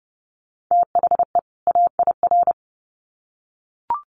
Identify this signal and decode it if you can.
Morse code, heard as a keyed tone, about 30 wpm, 700 Hz. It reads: THE AIR
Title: Sets of 2 Words 30wpm